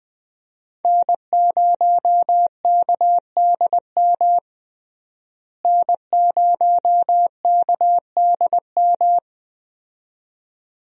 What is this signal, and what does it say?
Morse code, heard as a keyed tone, about 20 wpm, 700 Hz. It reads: N0KDM N0KDM
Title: USA Call Signs - 5 Characters - Repeated 2x - 20wpm